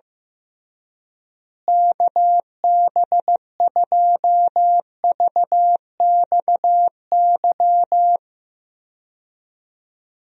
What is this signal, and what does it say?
Morse code, heard as a keyed tone, about 15 wpm, 700 Hz. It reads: KB2VXY
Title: USA Call Signs - 6 Characters - Repeated 3x - 15wpm